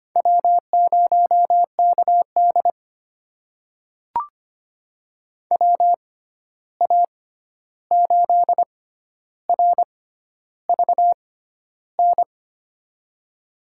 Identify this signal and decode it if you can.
Morse code, heard as a keyed tone, about 25 wpm, spaced as for 11 wpm, 700 Hz. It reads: W0KD WA8RVN
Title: USA Call Signs - 6x Character Spacing - 25wpm